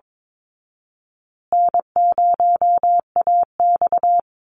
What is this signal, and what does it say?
Morse code, heard as a keyed tone, about 22 wpm, 700 Hz. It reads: N0AX
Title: Easy Call Signs Repeated 2x - 22wpm